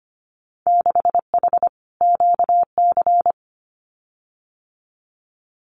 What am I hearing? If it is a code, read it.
Morse code, heard as a keyed tone, about 25 wpm, 700 Hz. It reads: 6H QC
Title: Winter Field Day Exchanges Encore 25wpm